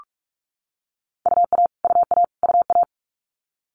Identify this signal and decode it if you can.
Morse code, heard as a keyed tone, about 46 wpm, 700 Hz. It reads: UA UA UA
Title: Multi-letter Phonemes Repeated 3x 45wpm